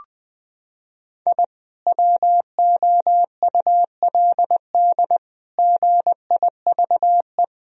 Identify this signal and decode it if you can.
Morse code, heard as a keyed tone, about 21 wpm, 700 Hz. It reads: I WOULD GIVE
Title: Sentences from Top 1000 Words - Review Part 1 - 20wpm